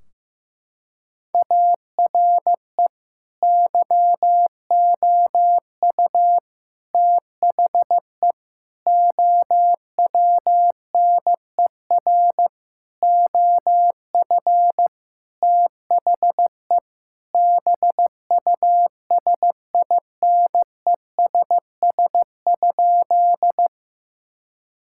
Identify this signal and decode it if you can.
Morse code, heard as a keyed tone, about 15 wpm, 700 Hz. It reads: ARE YOU THE OWNER OF THE BUSINESS?
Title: Sentences from Top 1600 Words 15wpm